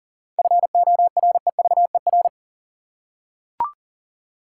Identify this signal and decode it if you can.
Morse code, heard as a keyed tone, about 40 wpm, 700 Hz. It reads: FOREVER
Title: Seven Letter Words - 40wpm